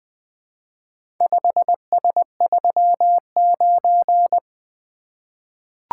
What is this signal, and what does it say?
Morse code, heard as a keyed tone, about 20 wpm, 700 Hz. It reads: 5S39
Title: Four Letter-Number Combinations - 20wpm